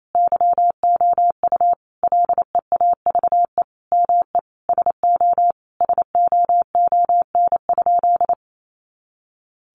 Morse code, heard as a keyed tone, about 28 wpm, 700 Hz. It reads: YOU LEAVE ME SO SOON?
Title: Sentences from Top 400 Words 28wpm